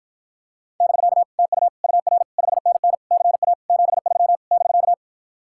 Morse code, heard as a keyed tone, about 52 wpm, 700 Hz. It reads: <BK> TU UR 5NN CA 73 <BK>